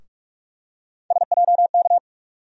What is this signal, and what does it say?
Morse code, heard as a keyed tone, about 46 wpm, 700 Hz. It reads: SJK
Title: USA Call Sign Postfixes - 3 Letters 45wpm